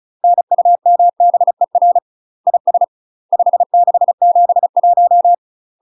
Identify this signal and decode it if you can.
Morse code, heard as a keyed tone, about 35 wpm, 700 Hz. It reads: NUMBER IS 5671
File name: UK MASTER QSOs TRAINING / 35WPM / UK MASTER QSOs 0023